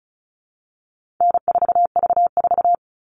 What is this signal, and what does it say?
Morse code, heard as a keyed tone, about 34 wpm, 700 Hz. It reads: N4V4
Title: All Cut Numbers - 4-Digit 35wpm